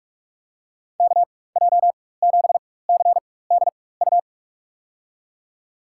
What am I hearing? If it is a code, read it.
Morse code, heard as a keyed tone, about 47 wpm, spaced as for 21 wpm, 700 Hz. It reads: KJ7CDU
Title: USA Call Signs - 4x Character Spacing - 45wpm